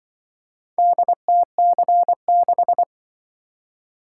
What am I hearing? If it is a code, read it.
Morse code, heard as a keyed tone, about 24 wpm, 700 Hz. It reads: DTC6
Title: Speed-Racing - Four Letter-Number Combinations - 20wpm